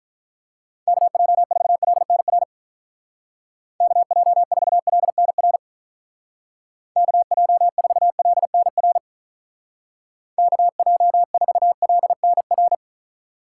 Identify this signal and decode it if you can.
Morse code, heard as a keyed tone, about 46 wpm, 700 Hz. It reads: KJ4LNR KJ4LNR KJ4LNR KJ4LNR